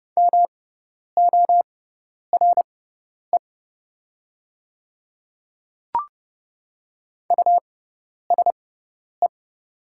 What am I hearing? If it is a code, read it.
Morse code, heard as a keyed tone, about 30 wpm, spaced as for 10 wpm, 700 Hz. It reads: MORE USE
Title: 6x Character Spacing - Top 200 Words - 30wpm